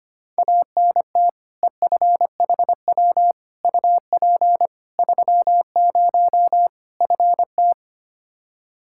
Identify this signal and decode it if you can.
Morse code, heard as a keyed tone, about 25 wpm, 700 Hz. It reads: ANT EFHW UP 30 FT
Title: Practice QSO - Exchange 2 - Info - 25wpm